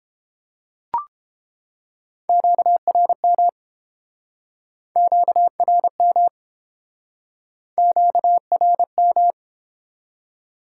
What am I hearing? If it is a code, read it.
Morse code, heard as a keyed tone, about 30 wpm, 700 Hz. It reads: QRM QRM QRM